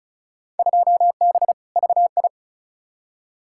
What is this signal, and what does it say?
Morse code, heard as a keyed tone, about 36 wpm, 700 Hz. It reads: 2B VI
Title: Field Day Exchanges Encore - 35wpm